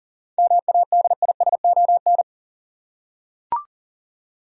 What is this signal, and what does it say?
Morse code, heard as a keyed tone, about 40 wpm, 700 Hz. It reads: MADISON
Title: USA Common Names - 40wpm